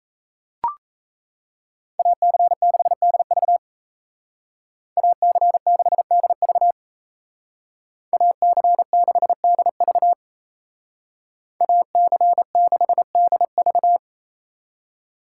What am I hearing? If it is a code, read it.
Morse code, heard as a keyed tone, about 33 wpm, 700 Hz. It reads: AC6DV AC6DV AC6DV AC6DV
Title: Speed Racing - USA 2x2 Call Signs 28wpm